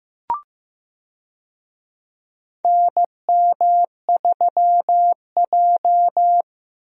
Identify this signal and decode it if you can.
Morse code, heard as a keyed tone, about 15 wpm, 700 Hz. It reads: NM3J
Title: USA Call Signs - 4 Characters - 15wpm